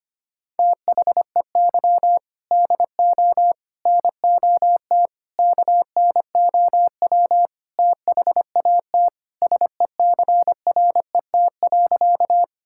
Morse code, heard as a keyed tone, about 25 wpm, 700 Hz. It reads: THEY DO NOT KNOW THAT SECRET.
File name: practice / 25wpm / Sentences with 4-Word Phrases - 25wpm